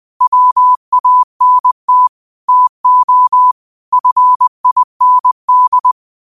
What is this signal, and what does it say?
Morse code, heard as a keyed tone, about 20 wpm, 1000 Hz. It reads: WANT TO FIND
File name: qotdfiles / 20wpm